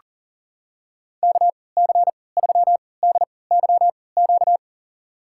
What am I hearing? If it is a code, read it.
Morse code, heard as a keyed tone, about 37 wpm, spaced as for 22 wpm, 700 Hz. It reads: KC3DYQ
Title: USA Call Signs - 6 Characters - 3x Character Spacing - 40wpm